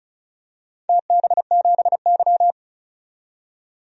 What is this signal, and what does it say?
Morse code, heard as a keyed tone, about 36 wpm, 700 Hz. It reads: TB7Y
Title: DXCC-100 Call Signs - 4 Characters - 35wpm